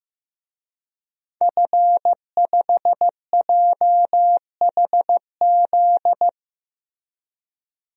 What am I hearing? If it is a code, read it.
Morse code, heard as a keyed tone, about 15 wpm, 700 Hz. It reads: F5JHZ
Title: DXCC-100 Call Signs - 5 Characters - Repeated 2x - 15wpm